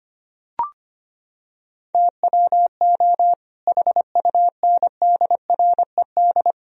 Morse code, heard as a keyed tone, about 25 wpm, 700 Hz. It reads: TWO HUNDRED
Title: Sentences from Top 1000 Words - Review Part 2 - 25wpm